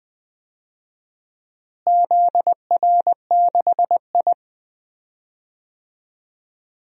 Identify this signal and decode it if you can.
Morse code, heard as a keyed tone, about 20 wpm, 700 Hz. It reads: ZR6I
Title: DXCC-100 Call Signs - 4 Characters - Repeated 3x - 20wpm